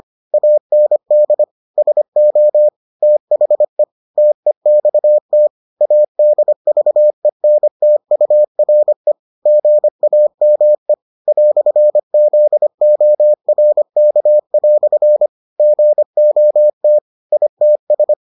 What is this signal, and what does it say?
Morse code, heard as a keyed tone, about 25 wpm, 600 Hz. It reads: AND SO THE TEXT ADVENTURE GAME "ZORK" GOT ITS